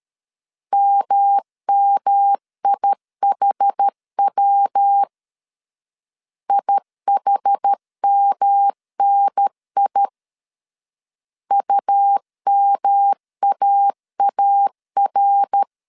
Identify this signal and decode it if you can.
Morse code, heard as a keyed tone, about 13 wpm, 800 Hz. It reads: MMIHW IHMNI UMAAR
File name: CW / DET / 1 3 1500C S C 12wpm 2750